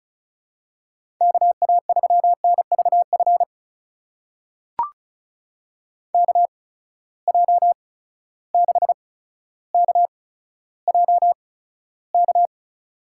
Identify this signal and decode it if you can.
Morse code, heard as a keyed tone, about 34 wpm, spaced as for 15 wpm, 700 Hz. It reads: KA3NVF KJ6KJK
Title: USA Call Signs - 6 Characters - 8x Character Spacing - 35wpm